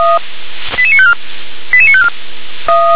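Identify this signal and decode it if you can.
Five-tone selective call (ZVEI), heard as quick runs of five tones, tones of about 70 ms. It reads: 88954 88954
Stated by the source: ZVEI five-tone selective call received by SDR.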